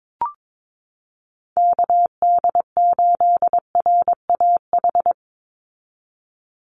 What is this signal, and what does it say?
Morse code, heard as a keyed tone, about 22 wpm, 700 Hz. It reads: KD8RAH
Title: USA Call Signs 22wpm